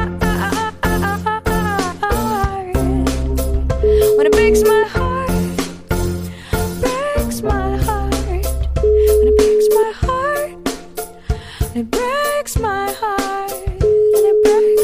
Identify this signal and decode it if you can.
Ringback tone (Europe, 425 Hz).